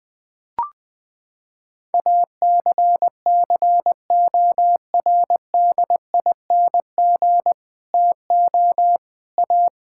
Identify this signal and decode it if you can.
Morse code, heard as a keyed tone, about 20 wpm, 700 Hz. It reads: ACCORDING TO A
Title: Sets of 8 Words - 20wpm